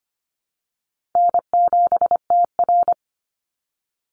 Morse code, heard as a keyed tone, about 25 wpm, 700 Hz. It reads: N7TR
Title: Call Signs Repeated 2x - 25wpm